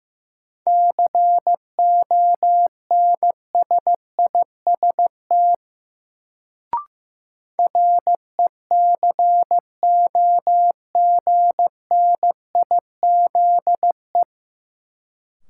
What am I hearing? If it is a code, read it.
Morse code, heard as a keyed tone, about 15 wpm, 700 Hz. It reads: CONSIST RECOGNIZE
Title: Sentences from Top 1400 Words - New Word Drill - 15wpm